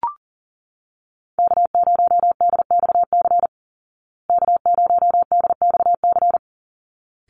Morse code, heard as a keyed tone, about 40 wpm, 700 Hz. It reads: K0DXC K0DXC
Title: Call Signs Repeated 3x - 40wpm